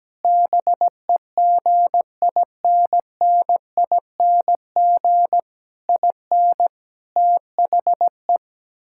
Morse code, heard as a keyed tone, about 17 wpm, 700 Hz. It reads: BEGINNING IN THE
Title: Sentences from Top 1000 Words - Review Part 6 - 17wpm